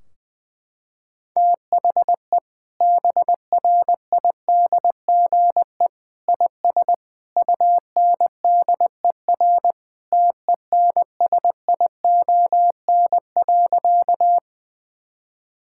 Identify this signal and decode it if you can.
Morse code, heard as a keyed tone, about 21 wpm, 700 Hz. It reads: THE BRIDGE IS UNDER TENSION.
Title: Sentences from Top 1600 Words 20wpm